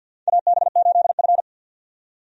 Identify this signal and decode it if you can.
Morse code, heard as a keyed tone, about 51 wpm, 700 Hz. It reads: AB8F
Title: USA Call Signs - 4 Characters - 50wpm